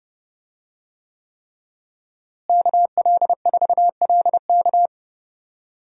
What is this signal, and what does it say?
Morse code, heard as a keyed tone, about 30 wpm, 700 Hz. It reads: KL4LK
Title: USA Call Signs - 5 Characters - Repeated 3x - 30wpm